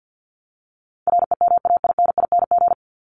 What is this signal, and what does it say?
Morse code, heard as a keyed tone, about 49 wpm, 700 Hz. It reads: REMAINING
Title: ING Suffix - Top 100 Words with Base Words 50wpm